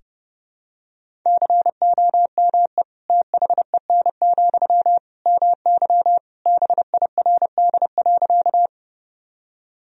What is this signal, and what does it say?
Morse code, heard as a keyed tone, about 30 wpm, 700 Hz. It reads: COME THEN, MY BIRD.
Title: Sentences from Top 1000 Words - Review Part 2 - 30wpm